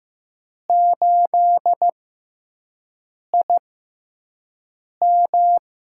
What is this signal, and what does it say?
Morse code, heard as a keyed tone, about 15 wpm, spaced as for 5 wpm, 700 Hz. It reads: 8IM
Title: USA Call Signs - 5 Characters - 6x Character Spacing - 15wpm